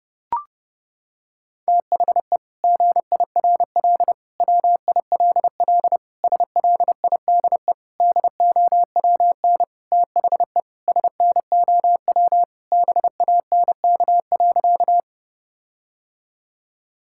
Morse code, heard as a keyed tone, about 30 wpm, 700 Hz. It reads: THE GIRL WILL SLIDE DOWN THE SNOW BANK.
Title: Sentences from Top 1600 Words 30wpm